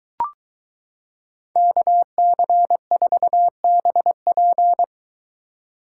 Morse code, heard as a keyed tone, about 23 wpm, 700 Hz. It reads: KC4BP